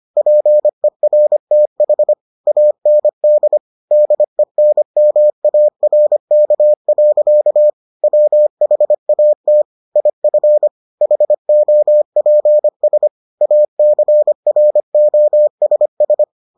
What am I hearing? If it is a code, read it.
Morse code, heard as a keyed tone, about 25 wpm, 600 Hz. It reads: PERTH AND DENMARK. WHAT IF HOPS ACROSS